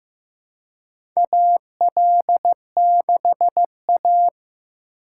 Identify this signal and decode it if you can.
Morse code, heard as a keyed tone, about 15 wpm, 700 Hz. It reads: AL6A
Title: USA Call Signs - 4 Characters - 4x Character Spacing - 15wpm